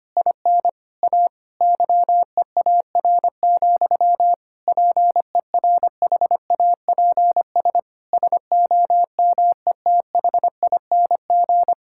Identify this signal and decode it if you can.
Morse code, heard as a keyed tone, about 25 wpm, 700 Hz. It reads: IN A YEAR, PERHAPS SOMETHING
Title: Sentences from Top 1600 Words 25wpm